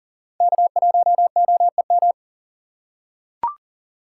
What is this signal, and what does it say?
Morse code, heard as a keyed tone, about 40 wpm, 700 Hz. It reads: K1OEM
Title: USA Call Signs - 5 Characters - 3x Character Spacing - 40wpm